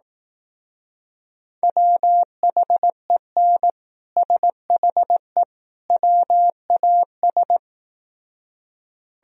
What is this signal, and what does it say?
Morse code, heard as a keyed tone, about 18 wpm, 700 Hz. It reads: WHEN SHE WAS
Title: Speed-Racing - Sets of 3 Words 15wpm